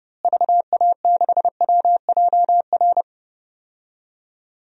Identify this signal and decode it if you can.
Morse code, heard as a keyed tone, about 31 wpm, 700 Hz. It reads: VA6WJR
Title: DXCC-100 Call Signs - 6 Characters - Repeated 2x - 30wpm